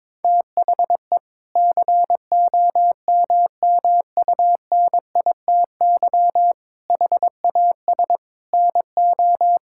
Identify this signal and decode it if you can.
Morse code, heard as a keyed tone, about 22 wpm, 700 Hz. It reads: THE COMMUNITY HAS NO
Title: Sentences from Top 1000 Words - Review Part 5 - 22wpm